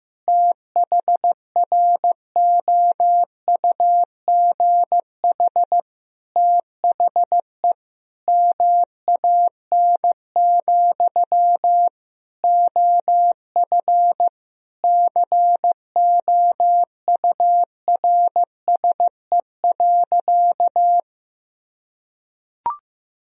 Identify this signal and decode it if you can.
Morse code, heard as a keyed tone, about 16 wpm, 700 Hz. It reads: THROUGH THE MAN, OF COURSE.
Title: Sentences from Top 1000 Words - Review Part 1 - 15wpm